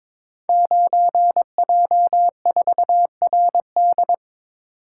Morse code, heard as a keyed tone, about 22 wpm, 700 Hz. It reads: 9J4RD
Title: Code Groups - 5 Letter:Numbers - 22wpm